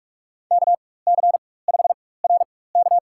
Morse code, heard as a keyed tone, about 46 wpm, spaced as for 21 wpm, 700 Hz. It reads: KC5RK
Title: USA Call Signs - 4x Character Spacing - 45wpm